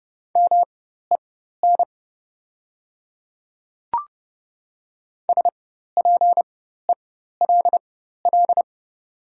Morse code, heard as a keyed tone, about 30 wpm, spaced as for 14 wpm, 690 Hz. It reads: MEN SPELL
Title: 4x Character Spacing - Top 200 Words - 30wpm